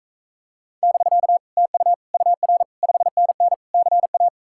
Morse code, heard as a keyed tone, about 42 wpm, 700 Hz. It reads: <BK> TU UR 5NN CA